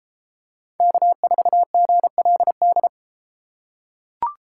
Morse code, heard as a keyed tone, about 32 wpm, 700 Hz. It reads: K4GLD